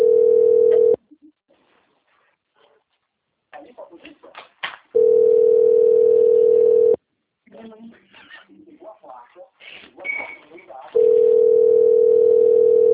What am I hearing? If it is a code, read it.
Ringback tone (North America), 440 + 480 Hz, with speech.
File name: out- - - - - .29523